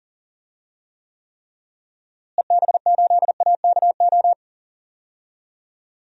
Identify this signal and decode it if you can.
Morse code, heard as a keyed tone, about 40 wpm, 700 Hz. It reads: EB8AKO
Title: DXCC-100 Call Signs - 6 Characters - Repeated 3x - 40wpm